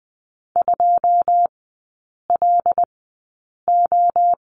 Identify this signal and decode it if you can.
Morse code, heard as a keyed tone, about 20 wpm, spaced as for 8 wpm, 690 Hz. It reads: 2LO
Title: 081 Lesson 14e - Number 2 - Three Characters 20wpm